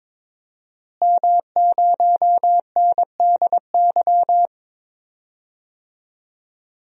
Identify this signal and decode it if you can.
Morse code, heard as a keyed tone, about 22 wpm, 700 Hz. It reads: M0NDY
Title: DXCC-100 Call Signs - 5 Characters - Repeated 3x - 22wpm